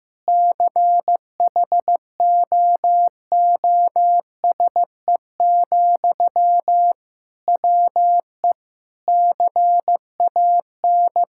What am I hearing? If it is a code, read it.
Morse code, heard as a keyed tone, about 16 wpm, 700 Hz. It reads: CHOOSE, WE CAN
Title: Sentences from Top 1000 Words - Review Part 6 - 15wpm